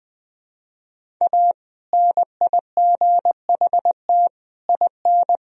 Morse code, heard as a keyed tone, about 20 wpm, 700 Hz. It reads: A NIGHT IN
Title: Sentences from Top 1000 Words - Review Part 1 - 20wpm